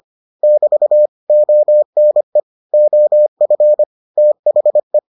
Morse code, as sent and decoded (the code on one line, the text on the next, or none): -...- / --- -. / --- ..-. / - .... .
<BT> ONE OF THE